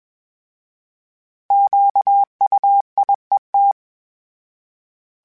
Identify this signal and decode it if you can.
Morse code, heard as a keyed tone, about 21 wpm, 800 Hz. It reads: QUIET
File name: Misc / Misc-mp3s / Misc @20